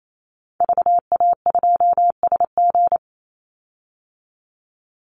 Morse code, heard as a keyed tone, about 28 wpm, 700 Hz. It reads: VA2SG
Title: Call Signs Repeated 2x - 28wpm